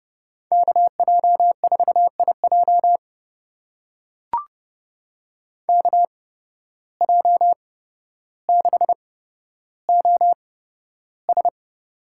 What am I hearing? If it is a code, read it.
Morse code, heard as a keyed tone, about 30 wpm, spaced as for 13 wpm, 700 Hz. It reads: KJ4IJ KJ6OS
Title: USA Call Signs - 5 Characters - 8x Character Spacing - 30wpm